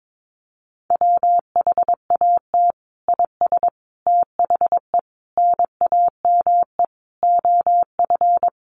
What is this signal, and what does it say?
Morse code, heard as a keyed tone, about 22 wpm, 700 Hz. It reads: WHAT IS THE NAME OF
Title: Sentences from Top 1500 Words 22wpm